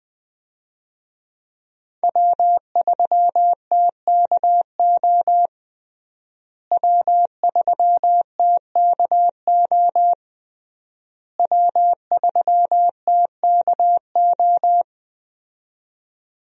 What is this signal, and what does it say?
Morse code, heard as a keyed tone, about 20 wpm, 700 Hz. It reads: W3TKO W3TKO W3TKO
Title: USA Call Signs - 5 Characters - Repeated 3x - 20wpm